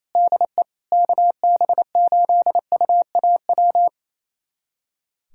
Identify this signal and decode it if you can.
Morse code, heard as a keyed tone, about 28 wpm, 700 Hz. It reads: DE KB8UAW
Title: USA Callsign DE Callsign 28wpm